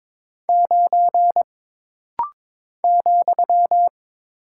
Morse code, heard as a keyed tone, about 22 wpm, 700 Hz. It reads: 9 ,
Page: This rapid-fire timing is geared to push your ICR (Instant Character Recognition) skills.